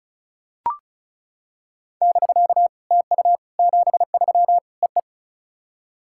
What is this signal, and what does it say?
Morse code, heard as a keyed tone, about 36 wpm, 700 Hz. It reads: <BK> TU 73 EE
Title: SOTA QSO Practice 35wpm